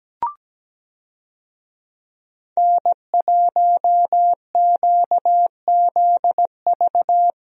Morse code, heard as a keyed tone, about 17 wpm, 700 Hz. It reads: N1QZV
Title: USA Call Signs - 5 Characters - Repeated 2x - 17wpm